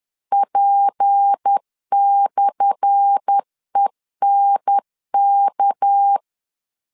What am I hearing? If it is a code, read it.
Morse code, heard as a keyed tone, about 10 wpm, 800 Hz. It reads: P/ENK
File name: LEZIONI / 10 SIGNS / 10A SLASH 10wpm 2808